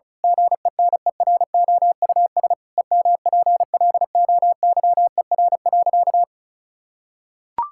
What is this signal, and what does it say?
Morse code, heard as a keyed tone, about 36 wpm, 700 Hz. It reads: GENEROUS EMPLOYER.
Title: Pangram Challenge v2 - 35wpm